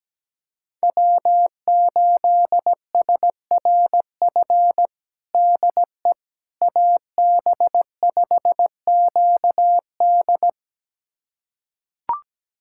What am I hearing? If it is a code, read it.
Morse code, heard as a keyed tone, about 17 wpm, 700 Hz. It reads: W8SRF DE AB5QD
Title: USA Callsign DE Callsign - 5 Characters - 17wpm